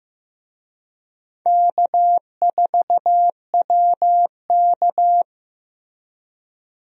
Morse code, heard as a keyed tone, about 15 wpm, 700 Hz. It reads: K4WK
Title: USA Call Signs - 4 Characters - Repeated 2x - 15wpm